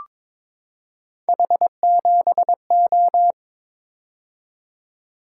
Morse code, heard as a keyed tone, about 21 wpm, 700 Hz. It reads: H7O
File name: practice / 22wpm / Three Letter-Number Combinations - 22wpm